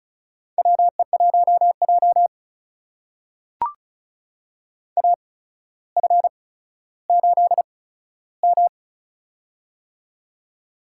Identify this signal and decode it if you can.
Morse code, heard as a keyed tone, about 36 wpm, spaced as for 15 wpm, 700 Hz. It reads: WE1J AF8M
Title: USA Call Signs - 4 Characters - 8x Character Spacing - 35wpm